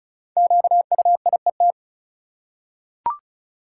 Morse code, heard as a keyed tone, about 36 wpm, 700 Hz. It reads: QUIET
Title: Five Letter Words - 35wpm